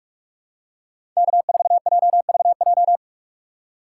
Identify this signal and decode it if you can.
Morse code, heard as a keyed tone, about 42 wpm, 700 Hz. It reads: K4JVJ
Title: USA Call Signs - 5 Characters - 4x Character Spacing - 45wpm